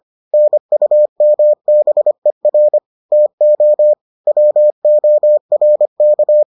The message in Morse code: NUMBER TO WORK

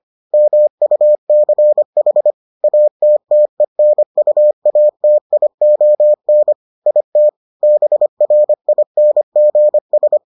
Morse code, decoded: MUCH ATTENUATION IT BRINGS

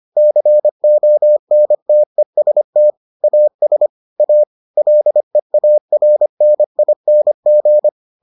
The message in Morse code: CONTEST AS A LEARNING

